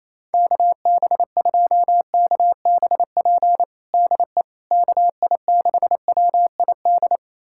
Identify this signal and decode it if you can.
Morse code, heard as a keyed tone, about 27 wpm, 700 Hz. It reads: KB2KBP DE KI6WID
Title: USA Callsign DE Callsign - 6 Characters - 28wpm